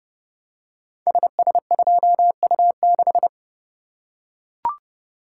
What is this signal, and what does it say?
Morse code, heard as a keyed tone, about 30 wpm, 700 Hz. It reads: SS2U6
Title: Code Groups - 5 Letter:Numbers - 30wpm